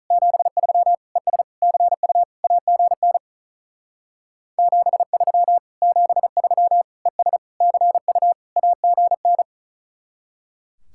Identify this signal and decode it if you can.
Morse code, heard as a keyed tone, about 43 wpm, 700 Hz. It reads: 73 ES CU AGN 73 73 ES CU AGN
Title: Speed-Racing Common QSO Phrases Encore - 35wpm